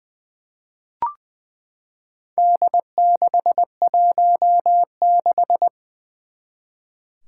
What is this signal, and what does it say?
Morse code, heard as a keyed tone, about 20 wpm, 700 Hz. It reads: D616
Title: Four Challenging Letter-Number Combinations 20wpm